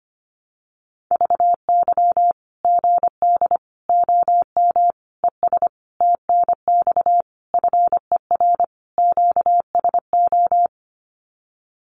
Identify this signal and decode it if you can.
Morse code, heard as a keyed tone, about 25 wpm, 700 Hz. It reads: VY GD OM ES TNX FER QSO
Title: Common QSO Phrases 25wpm